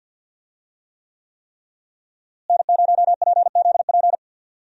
Morse code, heard as a keyed tone, about 50 wpm, 700 Hz. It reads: N0PZP
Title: USA Call Signs - 5 Characters - Repeated 3x - 50wpm